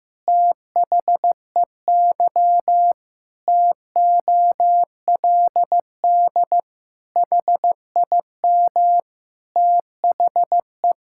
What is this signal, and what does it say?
Morse code, heard as a keyed tone, about 15 wpm, 700 Hz. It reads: THEY TOLD HIM THE
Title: Pangram Challenge v2 - 15wpm